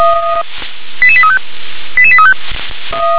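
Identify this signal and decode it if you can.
Five-tone selective call (ZVEI), heard as quick runs of five tones, tones of about 70 ms. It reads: 88835 88835
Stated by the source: ZVEI five-tone selective call 88100 received by SDR.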